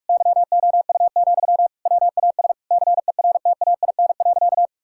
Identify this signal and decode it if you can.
Morse code, heard as a keyed tone, about 42 wpm, 700 Hz. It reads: YOU, WAS CERTAIN.
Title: Sentences from Top 1000 Words - Review Part 6 - 45wpm